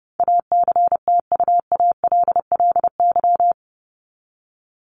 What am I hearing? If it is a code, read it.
Morse code, heard as a keyed tone, about 30 wpm, 700 Hz. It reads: ACTUALLY
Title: LY Suffix - Top 100 Words with Base Words 30wpm